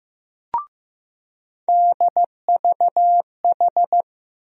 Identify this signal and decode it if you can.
Morse code, heard as a keyed tone, about 15 wpm, 700 Hz. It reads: DVH
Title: USA Call Sign Postfixes - 3 Letters 15wpm